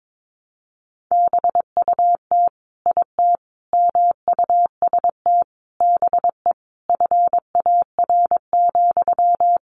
Morse code, heard as a keyed tone, about 22 wpm, 700 Hz. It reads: BUT IT MUST BE FAR,
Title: Sentences from Top 600 Words 22wpm